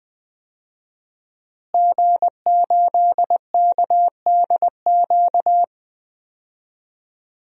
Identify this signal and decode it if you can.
Morse code, heard as a keyed tone, about 20 wpm, 700 Hz. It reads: G8KDQ
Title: DXCC-100 Call Signs - 5 Characters - Repeated 2x - 20wpm